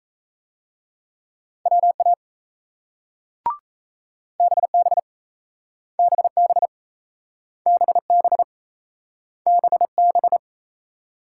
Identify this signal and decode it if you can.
Morse code, heard as a keyed tone, about 38 wpm, 700 Hz. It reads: WA BB BB BB BB